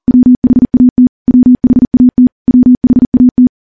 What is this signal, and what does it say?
Morse code, heard as a keyed tone, about 40 wpm, 250 Hz. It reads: WHAT WHAT WHAT
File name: morse / morse-training-40WPM